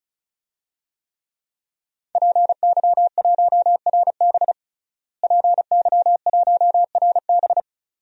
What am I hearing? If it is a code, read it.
Morse code, heard as a keyed tone, about 34 wpm, 700 Hz. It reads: PY1RB PY1RB
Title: DXCC-100 Call Signs - 5 Characters - Repeated 2x - 35wpm